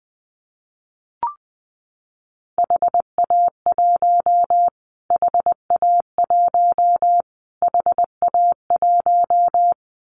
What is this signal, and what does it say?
Morse code, heard as a keyed tone, about 20 wpm, 700 Hz. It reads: HA1 HA1 HA1
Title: Callsign Prefixes for Top 25 DXCC - Repeated 3x - 20wpm